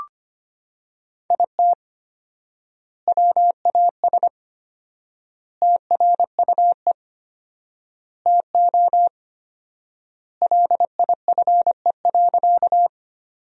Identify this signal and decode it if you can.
Morse code, heard as a keyed tone, about 25 wpm, 700 Hz. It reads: IT WAS TRUE TO LIFE.
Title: 4x Word-Spacing - Sentences from Top 400 Words - 25wpm